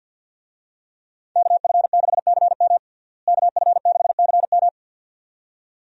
Morse code, heard as a keyed tone, about 51 wpm, 700 Hz. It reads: KF6CM KF6CM
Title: USA Call Signs - 5 Characters - Repeated 2x - 50wpm